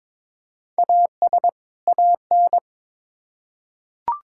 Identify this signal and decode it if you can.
Morse code, heard as a keyed tone, about 22 wpm, 700 Hz. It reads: AS AN
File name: Build A Phrase - 2-Word Phrases - 22wpm